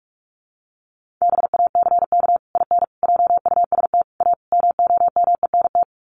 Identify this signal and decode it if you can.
Morse code, heard as a keyed tone, about 44 wpm, 700 Hz. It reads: BACK IN JUST A MOMENT
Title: Sets of 5 Words Encore 45wpm